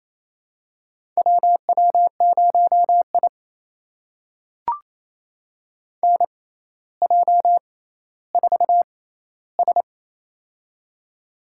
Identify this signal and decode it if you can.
Morse code, heard as a keyed tone, about 28 wpm, spaced as for 14 wpm, 700 Hz. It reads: WW0I NJ4S
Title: USA Call Signs - 4 Characters - 6x Character Spacing - 28wpm